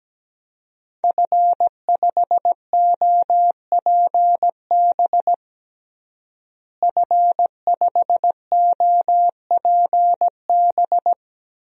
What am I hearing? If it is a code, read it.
Morse code, heard as a keyed tone, about 17 wpm, 700 Hz. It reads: F5OPB F5OPB
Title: DXCC-100 Call Signs - 5 Characters - Repeated 2x - 17wpm